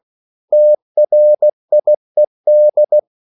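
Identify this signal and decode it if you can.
Morse code, heard as a keyed tone, about 16 wpm, 600 Hz. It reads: TRIED
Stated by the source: English quotations sent in CW as practice at 16 words per minute.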